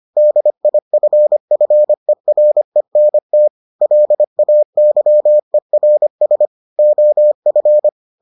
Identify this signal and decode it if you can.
Morse code, heard as a keyed tone, about 25 wpm, 600 Hz. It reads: DIFFERENT LAYERS OF